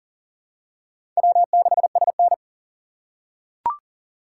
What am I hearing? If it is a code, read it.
Morse code, heard as a keyed tone, about 40 wpm, 700 Hz. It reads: W6SN
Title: USA Call Signs - 3x Character Spacing - 40wpm